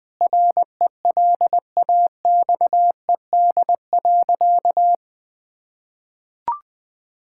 Morse code, heard as a keyed tone, about 20 wpm, 700 Hz. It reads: RELAXED.
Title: Sentences with 5-Word Phrases - 20wpm